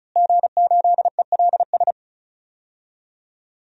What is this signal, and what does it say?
Morse code, heard as a keyed tone, about 36 wpm, 700 Hz. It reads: G8ELS
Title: DXCC-100 Call Signs - 5 Characters - Repeated 2x - 35wpm